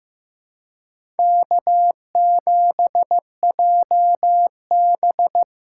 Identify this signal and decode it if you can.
Morse code, heard as a keyed tone, about 18 wpm, 700 Hz. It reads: K7JB
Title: Speed Racing - USA 1x2 Call Signs 15wpm